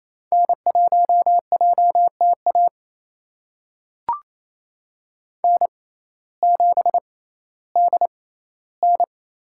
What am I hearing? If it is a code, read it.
Morse code, heard as a keyed tone, about 28 wpm, spaced as for 15 wpm, 700 Hz. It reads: N1JTA N7DN
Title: USA Call Signs - 5 Characters - 6x Character Spacing - 28wpm